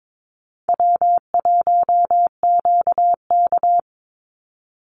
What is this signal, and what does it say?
Morse code, heard as a keyed tone, about 22 wpm, 700 Hz. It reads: W1QK
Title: Easy Call Signs Repeated 3x - 22wpm